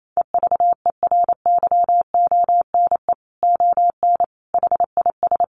Morse code, heard as a keyed tone, about 28 wpm, 700 Hz. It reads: EVERYONE ON HIS
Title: A Sound Of Thunder 28wpm